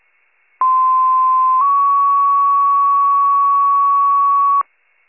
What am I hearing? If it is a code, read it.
Two-tone page, heard as a one-second tone, then a three-second tone, about 1040 then 1120 Hz. It is heard from 0.6 to 4.6 s.